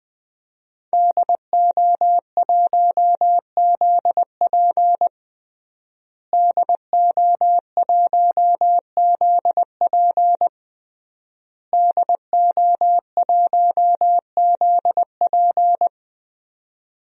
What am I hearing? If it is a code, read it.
Morse code, heard as a keyed tone, about 20 wpm, 700 Hz. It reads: DO1ZP DO1ZP DO1ZP
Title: DXCC-100 Call Signs - 5 Characters - Repeated 3x - 20wpm